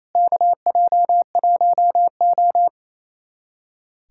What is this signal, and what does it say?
Morse code, heard as a keyed tone, about 28 wpm, 700 Hz. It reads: KJ1O
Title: USA Callsign DE Callsign - 4 Characters - 28wpm